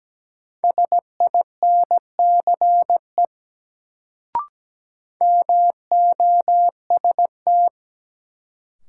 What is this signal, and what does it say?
Morse code, heard as a keyed tone, about 17 wpm, 700 Hz. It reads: SINCE MOST
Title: Sets of 2 Words - New Word Drill - 17wpm